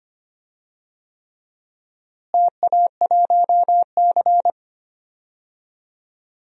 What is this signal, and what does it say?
Morse code, heard as a keyed tone, about 25 wpm, 700 Hz. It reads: TA1C
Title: DXCC-100 Call Signs - 4 Characters - Repeated 3x - 25wpm